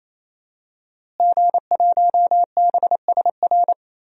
Morse code, heard as a keyed tone, about 27 wpm, 700 Hz. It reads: G1BSR
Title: DXCC-100 Call Signs - 5 Characters - Repeated 3x - 28wpm